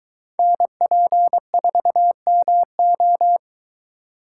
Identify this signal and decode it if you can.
Morse code, heard as a keyed tone, about 22 wpm, 700 Hz. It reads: NP4MO